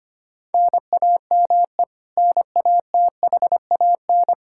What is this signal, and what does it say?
Morse code, heard as a keyed tone, about 25 wpm, 700 Hz. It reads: NAME NATHAN
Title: Practice QSO - Exchange 1 - Beginning - 25wpm